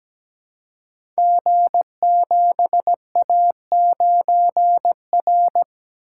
Morse code, heard as a keyed tone, about 17 wpm, 700 Hz. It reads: G7A9R
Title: Code Groups - 5 Letter:Numbers - 17wpm